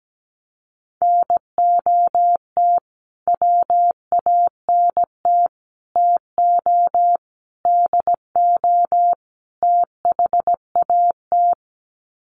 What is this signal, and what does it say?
Morse code, heard as a keyed tone, about 17 wpm, 700 Hz. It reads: NOT WANT TO DO THAT
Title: Sets of 5 Words Encore 17wpm